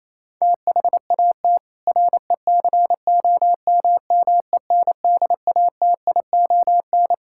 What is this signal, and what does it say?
Morse code, heard as a keyed tone, about 28 wpm, 700 Hz. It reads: THAT RECOMMENDATION
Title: Sentences from Top 1600 Words 28wpm